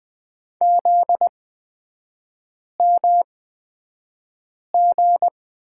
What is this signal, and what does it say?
Morse code, heard as a keyed tone, about 20 wpm, spaced as for 5 wpm, 700 Hz. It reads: ZMG
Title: 243 Lesson 39e - Letter z - Three Characters 20wpm